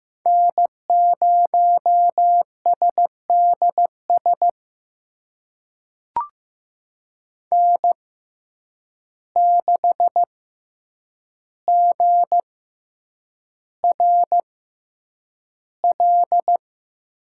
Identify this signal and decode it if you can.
Morse code, heard as a keyed tone, about 15 wpm, spaced as for 8 wpm, 700 Hz. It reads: N0SDS N6GRL
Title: USA Call Signs - 6x Character Spacing - 15wpm